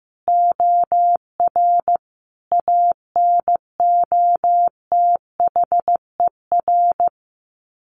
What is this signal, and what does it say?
Morse code, heard as a keyed tone, about 15 wpm, 700 Hz. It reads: OR ANOTHER
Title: Sets of 5 Words Encore 15wpm